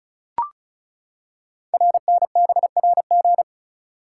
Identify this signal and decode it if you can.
Morse code, heard as a keyed tone, about 36 wpm, 700 Hz. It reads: RNBRG
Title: Code Groups - 5 Letters - 35wpm